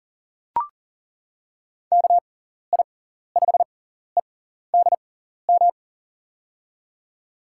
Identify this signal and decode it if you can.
Morse code, heard as a keyed tone, about 41 wpm, spaced as for 14 wpm, 700 Hz. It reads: KI5EDM
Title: USA Call Signs - 6 Characters - 6x Character Spacing - 40wpm